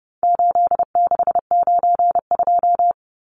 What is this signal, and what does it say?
Morse code, heard as a keyed tone, about 30 wpm, 700 Hz. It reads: 8692
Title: Numbers - 4-Digit 30wpm